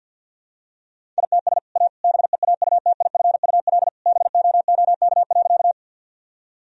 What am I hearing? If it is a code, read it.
Morse code, heard as a keyed tone, about 51 wpm, 700 Hz. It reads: ITS A BEAUTIFUL BOOK.
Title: Sentences from Top 1000 Words - Review Part 1 - 50wpm